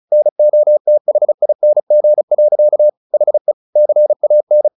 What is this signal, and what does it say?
Morse code, heard as a keyed tone, about 34 wpm, 600 Hz. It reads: NOTHING. HE CAN
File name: the old man and the sea / the old man and the sea 1 35wpm